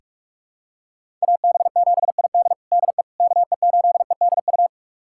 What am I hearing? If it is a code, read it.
Morse code, heard as a keyed tone, about 43 wpm, 700 Hz. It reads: AB7ID DE KE8EDU
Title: USA Callsign DE Callsign 45wpm